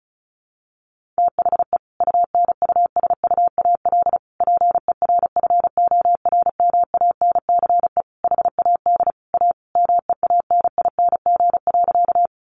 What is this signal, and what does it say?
Morse code, heard as a keyed tone, about 34 wpm, 700 Hz. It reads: THE UNUSUAL PERFORMANCE HAD A MEANING.
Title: Sentences from Top 1400 Words 35wpm